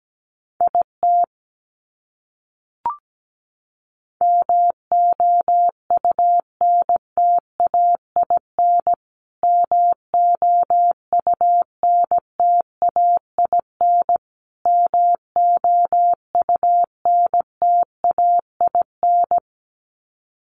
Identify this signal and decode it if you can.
Morse code, heard as a keyed tone, about 17 wpm, 700 Hz. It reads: IT MOUNTAIN MOUNTAIN MOUNTAIN
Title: Top 500 Words Repeated 17wpm